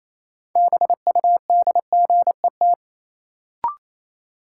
Morse code, heard as a keyed tone, about 28 wpm, 700 Hz. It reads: BUDGET
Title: Sentences from Top 1200 Words - New Word Drill - 28wpm